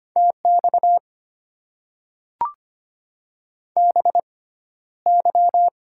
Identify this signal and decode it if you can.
Morse code, heard as a keyed tone, about 25 wpm, spaced as for 13 wpm, 700 Hz. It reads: TX BY
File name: Top 90 QSO Elements - 6x Character Spacing - 25wpm